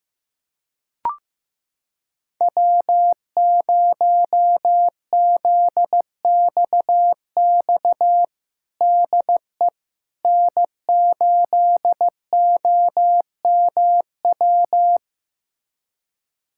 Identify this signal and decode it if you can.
Morse code, heard as a keyed tone, about 15 wpm, 700 Hz. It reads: W0ZXX DE N8OMW
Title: USA Callsign DE Callsign - 5 Characters - 15wpm